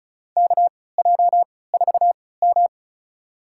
Morse code, heard as a keyed tone, about 34 wpm, spaced as for 20 wpm, 700 Hz. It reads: KJ4M
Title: USA Call Signs - 4 Characters - 3x Character Spacing - 35wpm